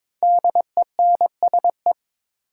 Morse code, heard as a keyed tone, about 22 wpm, 700 Hz. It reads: DENSE
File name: 5 Letter Words - First 3 Letters Given - 22wpm